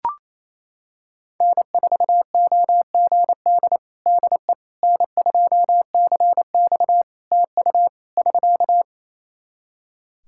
Morse code, heard as a keyed tone, about 28 wpm, 700 Hz. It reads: N4OGD DE N2CX TU <SK>
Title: Practice QSO - Exchange 4 - Ending - 28wpm